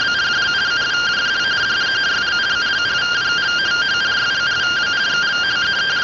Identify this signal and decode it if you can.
Two-tone teleprinter signal, about 170 Hz shift.